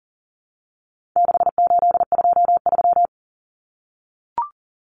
Morse code, heard as a keyed tone, about 40 wpm, 700 Hz. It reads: 6823
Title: Numbers - 4-Digit 40wpm